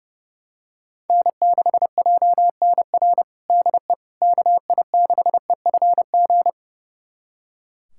Morse code, heard as a keyed tone, about 30 wpm, 700 Hz. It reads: N6JNR DE KI6EFG
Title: USA Callsign DE Callsign 30wpm